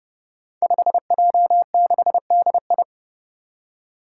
Morse code, heard as a keyed tone, about 30 wpm, 700 Hz. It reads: 5J6DI
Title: Five Challenging Letter-Number Combinations 30wpm